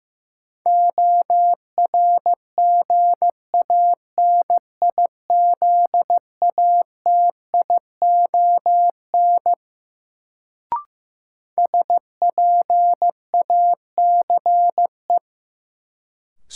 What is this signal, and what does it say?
Morse code, heard as a keyed tone, about 15 wpm, 700 Hz. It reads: ORGANIZATION SPACE
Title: Sentences from Top 1000 Words - Review Part 5 - New Word Drill - 15wpm